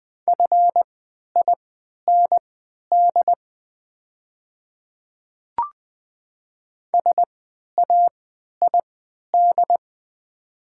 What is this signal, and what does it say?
Morse code, heard as a keyed tone, about 20 wpm, spaced as for 11 wpm, 690 Hz. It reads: FIND SAID
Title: 3x Character Spacing - Top 100 Words - 20wpm